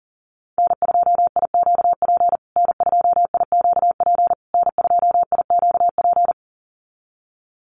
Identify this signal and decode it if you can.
Morse code, heard as a keyed tone, about 40 wpm, 700 Hz. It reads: N2IQP N2IQP N2IQP
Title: USA Call Signs Repeated 3x 40wpm